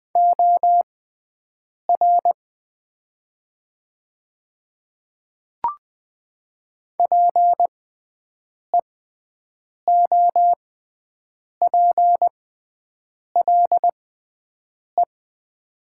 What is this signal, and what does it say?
Morse code, heard as a keyed tone, about 20 wpm, spaced as for 7 wpm, 700 Hz. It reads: OR PEOPLE
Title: 6x Character Spacing - Top 100 Words - 20wpm